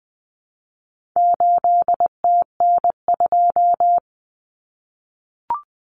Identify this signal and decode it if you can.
Morse code, heard as a keyed tone, about 20 wpm, 700 Hz. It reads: 8TN2
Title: Common Cut Numbers - 4-Digit 20wpm